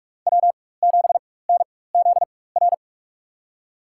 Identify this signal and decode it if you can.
Morse code, heard as a keyed tone, about 44 wpm, spaced as for 21 wpm, 700 Hz. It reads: W7NZR
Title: USA Call Signs - 5 Characters - 4x Character Spacing - 45wpm